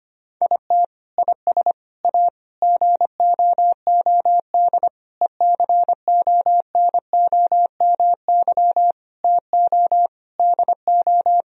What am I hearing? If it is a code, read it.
Morse code, heard as a keyed tone, about 25 wpm, 700 Hz. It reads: IT IS A GOOD ECONOMY TO DO
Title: Sentences from Top 1000 Words - Review Part 3 - 25wpm